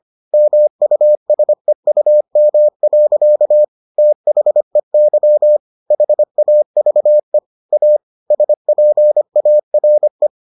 Morse code, heard as a keyed tone, about 25 wpm, 600 Hz. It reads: MUSEUM. THEY HAVE A SPARE